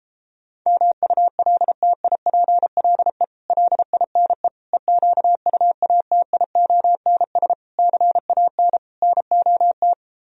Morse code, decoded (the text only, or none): MULTIPLE LINE EQUATIONS CAN NOT